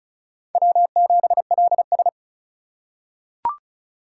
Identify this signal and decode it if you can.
Morse code, heard as a keyed tone, about 36 wpm, 700 Hz. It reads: W7LS
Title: USA Call Signs - 4x Character Spacing - 35wpm